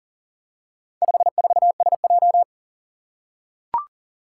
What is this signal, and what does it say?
Morse code, heard as a keyed tone, about 41 wpm, 700 Hz. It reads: 54SJ
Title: Four Challenging Letter-Number Combinations 40wpm